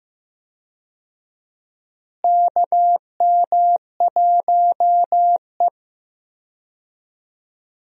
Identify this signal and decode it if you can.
Morse code, heard as a keyed tone, about 15 wpm, 700 Hz. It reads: KM1E
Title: USA Call Signs - 4 Characters - Repeated 2x - 15wpm